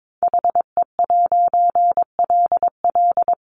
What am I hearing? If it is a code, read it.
Morse code, heard as a keyed tone, about 22 wpm, 700 Hz. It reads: HE'LL
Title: A Sound Of Thunder 22wpm